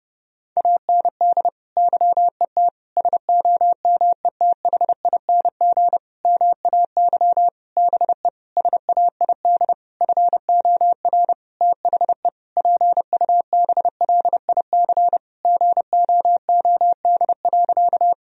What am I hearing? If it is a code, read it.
Morse code, heard as a keyed tone, about 30 wpm, 700 Hz. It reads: AND YET SOMETHING MAY BE SAID FOR THE PUBLIC GOOD.
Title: Sentences from Top 1000 Words - Review Part 3 - 30wpm